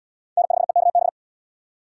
Morse code, heard as a keyed tone, about 76 wpm, 700 Hz. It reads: N5EKB